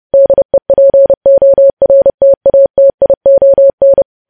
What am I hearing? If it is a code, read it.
Morse code, heard as a keyed tone, about 30 wpm, 550 Hz. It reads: DEPORTATION
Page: news headlines in morse code at 30 wpm